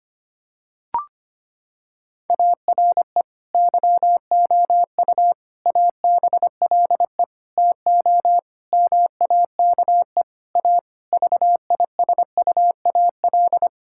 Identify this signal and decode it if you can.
Morse code, heard as a keyed tone, about 25 wpm, 700 Hz. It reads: ARE YOU ABLE TO MAKE A VISUAL
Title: Sentences from Top 1000 Words - Review Part 6 - 25wpm